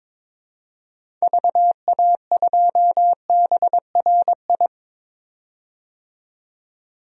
Morse code, heard as a keyed tone, about 21 wpm, 700 Hz. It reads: VA2BRI
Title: DXCC-100 Call Signs - 6 Characters - 22wpm